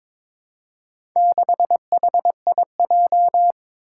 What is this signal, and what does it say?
Morse code, heard as a keyed tone, about 22 wpm, 700 Hz. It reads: 6HIJ
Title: Four Challenging Letter-Number Combinations 22wpm